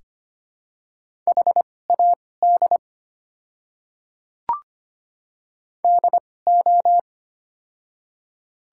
Morse code, heard as a keyed tone, about 25 wpm, spaced as for 18 wpm, 690 Hz. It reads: HAD DO
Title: 2x Character Spacing - Top 100 Words - 25wpm